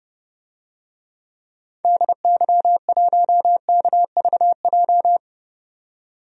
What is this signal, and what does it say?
Morse code, heard as a keyed tone, about 30 wpm, 700 Hz. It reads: DY1KVJ
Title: DXCC-100 Call Signs - 6 Characters - 30wpm